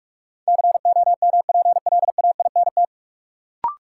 Morse code, heard as a keyed tone, about 46 wpm, 700 Hz. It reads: COMPLAINT
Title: Sentences from Top 1400 Words - New Word Drill - 45wpm